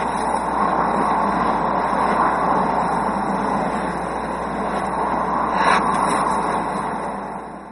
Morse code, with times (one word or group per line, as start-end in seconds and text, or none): none